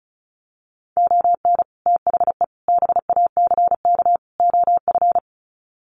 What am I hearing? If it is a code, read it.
Morse code, heard as a keyed tone, about 34 wpm, 700 Hz. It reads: ON THE BACK OF
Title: Sets of 4 Words 35wpm